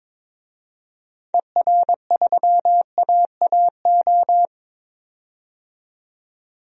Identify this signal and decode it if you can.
Morse code, heard as a keyed tone, about 22 wpm, 700 Hz. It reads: ER3AAO
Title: DXCC-100 Call Signs - 6 Characters - 22wpm